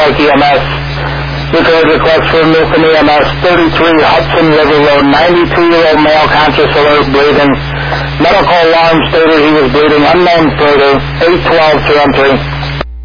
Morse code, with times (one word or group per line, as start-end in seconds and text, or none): none